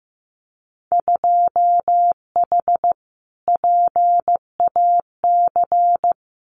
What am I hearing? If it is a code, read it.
Morse code, heard as a keyed tone, about 15 wpm, 700 Hz. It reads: 2H PAC
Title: Winter Field Day Exchanges Encore 15wpm